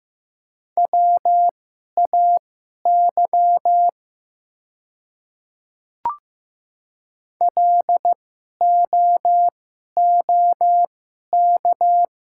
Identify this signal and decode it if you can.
Morse code, heard as a keyed tone, about 15 wpm, spaced as for 11 wpm, 700 Hz. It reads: WAY LOOK
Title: 2x Character Spacing - Top 100 Words - 15wpm